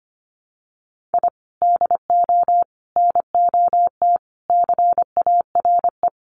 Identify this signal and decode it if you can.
Morse code, heard as a keyed tone, about 25 wpm, 700 Hz. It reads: I DO NOT CARE
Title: Sets of 4 Words 25wpm